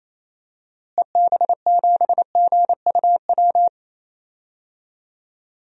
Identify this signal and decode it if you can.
Morse code, heard as a keyed tone, about 28 wpm, 700 Hz. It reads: EB7GUW
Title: DXCC-100 Call Signs - 6 Characters - Repeated 2x - 28wpm